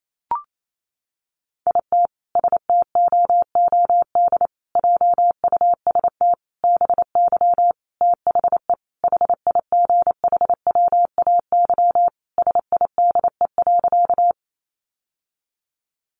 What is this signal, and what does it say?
Morse code, heard as a keyed tone, about 28 wpm, 700 Hz. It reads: IT STOOD JUST BY THE HIGHWAY SIDE.
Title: Sentences from Top 1400 Words 28wpm